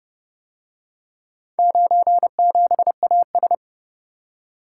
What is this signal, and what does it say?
Morse code, heard as a keyed tone, about 28 wpm, 700 Hz. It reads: 97AS